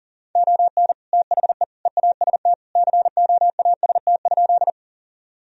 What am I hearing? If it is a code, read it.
Morse code, heard as a keyed tone, about 40 wpm, 700 Hz. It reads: ON THE EAST COAST?
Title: Sentences from Top 1000 Words - Review Part 4 - 40wpm